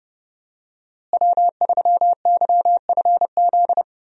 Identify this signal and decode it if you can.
Morse code, heard as a keyed tone, about 30 wpm, 700 Hz. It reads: W3YFZ
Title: USA Call Signs - 5 Characters - Repeated 3x - 30wpm